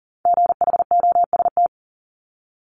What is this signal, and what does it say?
Morse code, heard as a keyed tone, about 40 wpm, 700 Hz. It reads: GHOST
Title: Multi-letter Phonemes Repeated 3x 40wpm